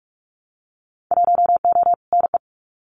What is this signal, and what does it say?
Morse code, heard as a keyed tone, about 44 wpm, 700 Hz. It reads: 1O NE
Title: Winter Field Day Exchanges Encore 45wpm